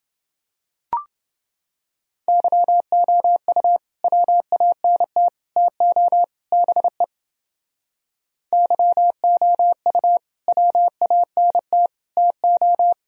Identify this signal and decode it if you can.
Morse code, heard as a keyed tone, about 30 wpm, 700 Hz. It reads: YOU WANT TO BE YOU WANT TO